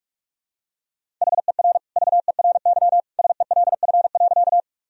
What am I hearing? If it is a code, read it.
Morse code, heard as a keyed tone, about 41 wpm, 700 Hz. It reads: HER VERY SELF.
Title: Sentences from Top 1000 Words - Review Part 1 - 45wpm